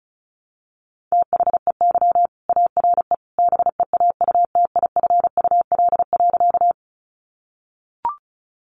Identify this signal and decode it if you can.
Morse code, heard as a keyed tone, about 34 wpm, 700 Hz. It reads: THEY ARE BEAUTIFUL.
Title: Sentences from Top 900 Words 35wpm